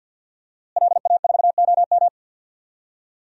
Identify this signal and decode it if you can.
Morse code, heard as a keyed tone, about 48 wpm, 700 Hz. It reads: LA4OM
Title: DXCC-100 Call Signs - 5 Characters - Repeated 2x - 50wpm